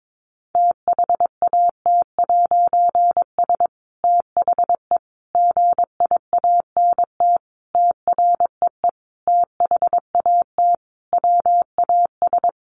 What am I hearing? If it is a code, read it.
Morse code, heard as a keyed tone, about 22 wpm, 700 Hz. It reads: THAT'S THE GIANT TREE THAT WAS
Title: A Sound Of Thunder 22wpm